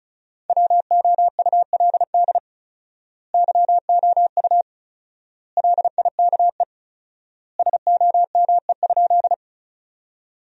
Morse code, heard as a keyed tone, about 36 wpm, 700 Hz. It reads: WOULD YOU LIKE SOME?
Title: 4x Word-Spacing - Sentences from Top 100 Words - 35wpm